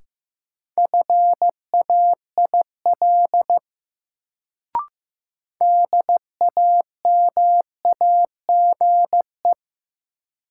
Morse code, heard as a keyed tone, about 15 wpm, 700 Hz. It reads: FAIL DAMAGE
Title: Sentences from Top 1100 Words - New Word Drill Alt - 15wpm